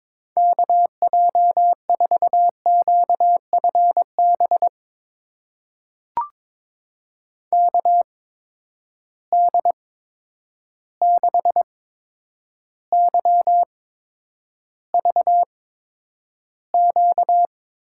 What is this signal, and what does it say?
Morse code, heard as a keyed tone, about 22 wpm, spaced as for 9 wpm, 700 Hz. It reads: KJ4QFB KD6YVQ
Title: USA Call Signs - 6 Characters - 8x Character Spacing - 22wpm